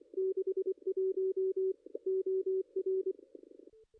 Morse code, heard as a keyed tone, about 24 wpm, 380 Hz. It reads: 61 OR
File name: contests / ss-cw-k5tr-op-n5rz -2300